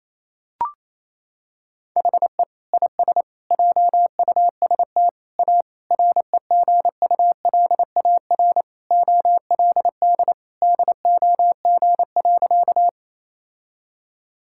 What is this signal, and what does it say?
Morse code, heard as a keyed tone, about 28 wpm, 700 Hz. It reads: HE IS JUST A REGULAR OLD DOG.
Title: Sentences from Top 1000 Words - Review Part 6 - 28wpm